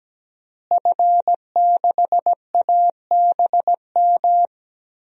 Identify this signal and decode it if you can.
Morse code, heard as a keyed tone, about 17 wpm, 700 Hz. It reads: F6ABM
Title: DXCC-100 Call Signs - 5 Characters - Repeated 3x - 17wpm